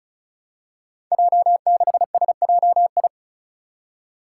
Morse code, heard as a keyed tone, about 36 wpm, 700 Hz. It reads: J6SJI
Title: Five Challenging Letter-Number Combinations 35wpm